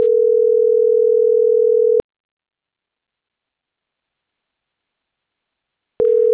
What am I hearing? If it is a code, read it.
Ringback tone (North America), 440 + 480 Hz.